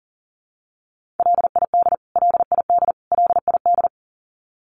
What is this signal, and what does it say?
Morse code, heard as a keyed tone, about 40 wpm, 700 Hz. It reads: LID LID LID
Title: Abbreviations Repeated 3x - 40wpm